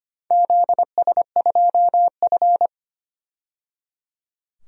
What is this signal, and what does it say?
Morse code, heard as a keyed tone, about 25 wpm, 700 Hz. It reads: ZS2F
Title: DXCC-100 Call Signs - 4 Characters - 25wpm